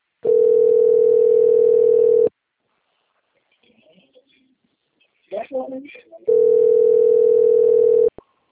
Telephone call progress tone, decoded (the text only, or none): ringback tone